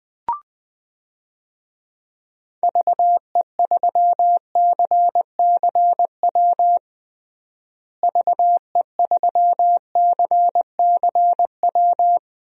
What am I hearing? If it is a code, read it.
Morse code, heard as a keyed tone, about 20 wpm, 700 Hz. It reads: VE3CCW VE3CCW
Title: DXCC-100 Call Signs - 6 Characters - Repeated 3x - 20wpm